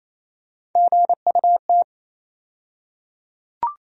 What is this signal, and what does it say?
Morse code, heard as a keyed tone, about 28 wpm, 700 Hz. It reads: GUT
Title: USA Call Sign Postfixes - 3 Letters 28wpm